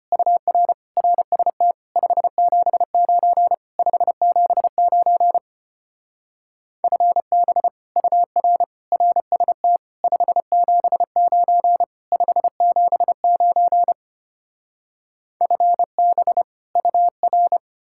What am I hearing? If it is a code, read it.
Morse code, heard as a keyed tone, about 33 wpm, 700 Hz. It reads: UR RST 579 579 FB UR RST 579 579 FB UR